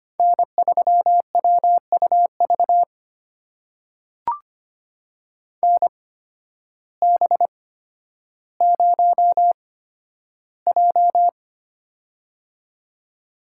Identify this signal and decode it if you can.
Morse code, heard as a keyed tone, about 25 wpm, spaced as for 12 wpm, 700 Hz. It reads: N3WUV NB0J
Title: USA Call Signs - 8x Character Spacing - 25wpm